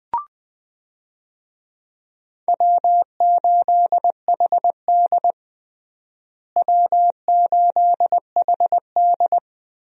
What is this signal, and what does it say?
Morse code, heard as a keyed tone, about 20 wpm, 700 Hz. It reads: W8HD W8HD
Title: USA Call Signs - 4 Characters - Repeated 3x - 20wpm